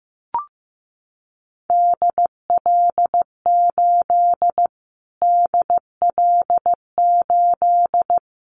Morse code, heard as a keyed tone, about 15 wpm, 700 Hz. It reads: DL8 DL8
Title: Callsign Prefixes for Top 25 DXCC - Repeated 3x - 15wpm